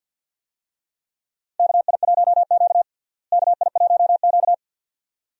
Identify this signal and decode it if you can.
Morse code, heard as a keyed tone, about 47 wpm, 700 Hz. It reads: KI1Q KI1Q
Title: USA Call Signs - 4 Characters - Repeated 2x - 50wpm